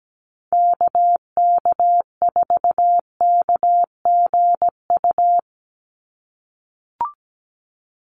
Morse code, heard as a keyed tone, about 17 wpm, 700 Hz. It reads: KK4KGU
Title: USA Call Signs 17wpm